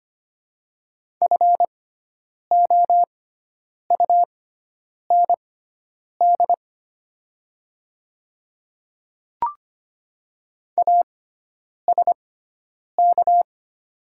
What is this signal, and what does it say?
Morse code, heard as a keyed tone, about 25 wpm, spaced as for 9 wpm, 700 Hz. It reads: FOUND ASK
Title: 6x Character Spacing - Top 300 Words - 25wpm